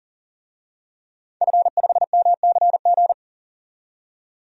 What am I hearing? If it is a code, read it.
Morse code, heard as a keyed tone, about 40 wpm, 700 Hz. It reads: F5MCG
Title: DXCC-100 Call Signs - 5 Characters - 40wpm